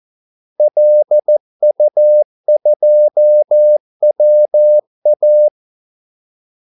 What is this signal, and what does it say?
Morse code, heard as a keyed tone, about 14 wpm, 600 Hz. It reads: LU2WA
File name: Callsigns-14wpm